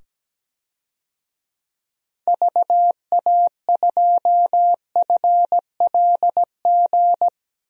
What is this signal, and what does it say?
Morse code, heard as a keyed tone, about 17 wpm, 700 Hz. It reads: VA2FLG
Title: DXCC-100 Call Signs - 6 Characters - 17wpm